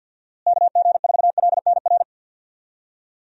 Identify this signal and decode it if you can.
Morse code, heard as a keyed tone, about 50 wpm, 700 Hz. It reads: KG4LNR
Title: USA Call Signs - 6 Characters - Repeated 3x - 50wpm